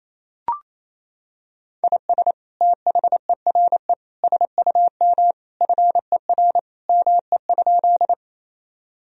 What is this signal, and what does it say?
Morse code, heard as a keyed tone, about 28 wpm, 700 Hz. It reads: IS THERE SUM FER ME?
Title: Sentences from Top 200 Words and Abbreviations 28wpm